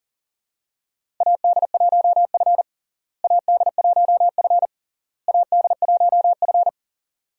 Morse code, heard as a keyed tone, about 40 wpm, 700 Hz. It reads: AD1F AD1F AD1F
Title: USA Call Signs - 4 Characters - Repeated 3x - 40wpm